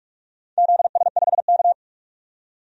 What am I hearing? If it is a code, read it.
Morse code, heard as a keyed tone, about 46 wpm, 700 Hz. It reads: ZS5K
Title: DXCC-100 Call Signs - 4 Characters - 45wpm